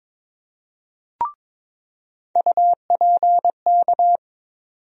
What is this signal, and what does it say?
Morse code, heard as a keyed tone, about 22 wpm, 700 Hz. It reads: UPK
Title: USA Call Sign Postfixes - 3 Letters 22wpm